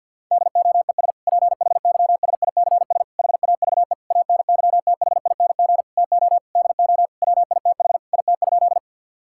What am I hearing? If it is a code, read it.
Morse code, heard as a keyed tone, about 50 wpm, 700 Hz. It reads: DOES PHYSICS HAVE ANYTHING TO DO WITH IT?
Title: Sentences from Top 1000 Words - Review Part 5 - 50wpm